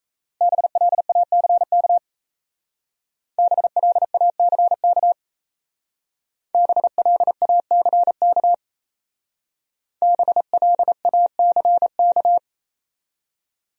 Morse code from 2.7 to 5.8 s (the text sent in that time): BLACK